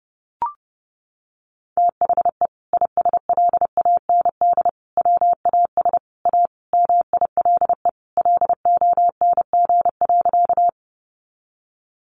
Morse code, heard as a keyed tone, about 30 wpm, 700 Hz. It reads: THE ISLAND WAS A MILE LONG.
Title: Sentences from Top 500 Words 30wpm